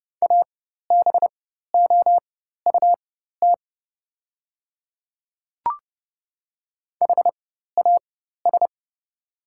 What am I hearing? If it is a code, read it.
Morse code, heard as a keyed tone, about 30 wpm, spaced as for 14 wpm, 700 Hz. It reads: ABOUT HAS
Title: 4x Character Spacing - Top 100 Words - 30wpm